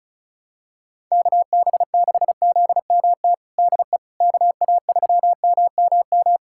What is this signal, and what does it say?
Morse code, heard as a keyed tone, about 36 wpm, 700 Hz. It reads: KB6ZMT DE KA3MMM
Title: USA Callsign DE Callsign 35wpm